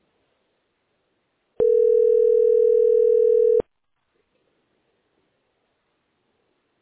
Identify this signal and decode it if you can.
Ringback tone (North America), 440 + 480 Hz.